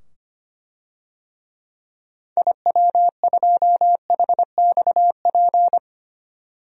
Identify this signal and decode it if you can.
Morse code, heard as a keyed tone, about 25 wpm, 700 Hz. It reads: IW2HXP
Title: DXCC-100 Call Signs - 6 Characters - 25wpm